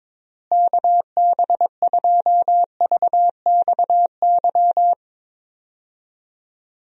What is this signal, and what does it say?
Morse code, heard as a keyed tone, about 22 wpm, 700 Hz. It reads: KB2VXY
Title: USA Call Signs - 6 Characters - Repeated 3x - 22wpm